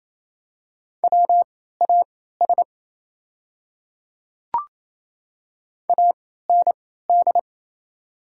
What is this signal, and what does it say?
Morse code, heard as a keyed tone, about 27 wpm, spaced as for 16 wpm, 700 Hz. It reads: WAS AND
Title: 3x Character Spacing - Top 100 Words - 28wpm